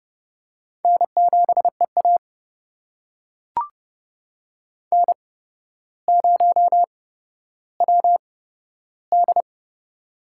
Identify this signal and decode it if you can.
Morse code, heard as a keyed tone, about 30 wpm, spaced as for 13 wpm, 700 Hz. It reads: N7EA N0WD
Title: USA Call Signs - 4 Characters - 8x Character Spacing - 30wpm